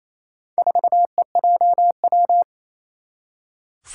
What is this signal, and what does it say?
Morse code, heard as a keyed tone, about 28 wpm, 700 Hz. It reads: 4EJW
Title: Four Challenging Letter-Number Combinations 28wpm